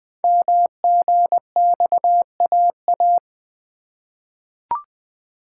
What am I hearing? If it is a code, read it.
Morse code, heard as a keyed tone, about 20 wpm, 700 Hz. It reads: MGXAA
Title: Code Groups - 5 Letters - 20wpm